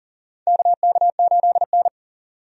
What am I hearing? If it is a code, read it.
Morse code, heard as a keyed tone, about 40 wpm, 700 Hz. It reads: KK8N
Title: USA Call Signs - 4 Characters - 40wpm